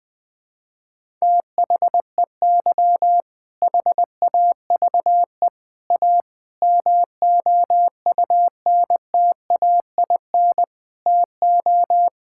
Morse code, heard as a keyed tone, about 20 wpm, 700 Hz. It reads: THEY HAVE A MOUNTAIN TO
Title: Sentences from Top 1000 Words - Review Part 5 - 20wpm